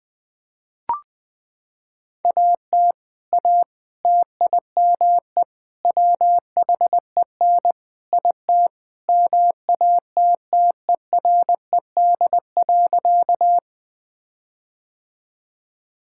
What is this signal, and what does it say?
Morse code, heard as a keyed tone, about 20 wpm, 700 Hz. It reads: AT A TIME WHEN IT MATTERED.
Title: Sentences with 4-Word Phrases - 20wpm